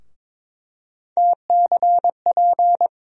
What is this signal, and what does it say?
Morse code, heard as a keyed tone, about 21 wpm, 700 Hz. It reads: TCP
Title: USA Call Sign Postfixes - 3 Letters 22wpm